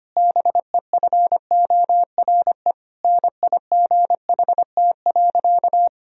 Morse code, heard as a keyed tone, about 25 wpm, 700 Hz. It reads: BEFORE NIGHT.
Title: Sentences from Top 1000 Words - Review Part 3 - 25wpm